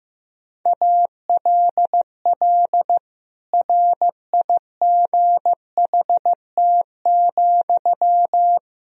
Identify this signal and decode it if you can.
Morse code, heard as a keyed tone, about 15 wpm, 700 Hz. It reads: ALL RIGHT,
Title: Sentences from Top 1000 Words - Review Part 3 - 15wpm